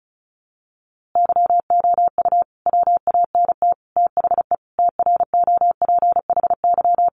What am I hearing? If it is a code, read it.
Morse code, heard as a keyed tone, about 34 wpm, 700 Hz. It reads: YOU WANT THE TROPHY
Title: A Sound Of Thunder 35wpm